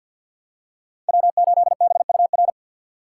Morse code, heard as a keyed tone, about 53 wpm, 700 Hz. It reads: W8BUR
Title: USA Call Signs - 2x Character Spacing - 50wpm